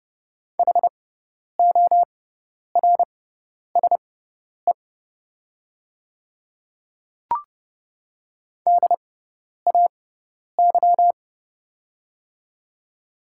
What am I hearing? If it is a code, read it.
Morse code, heard as a keyed tone, about 30 wpm, spaced as for 10 wpm, 700 Hz. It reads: HORSE DAY
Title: 6x Character Spacing - Top 500 Words - 30wpm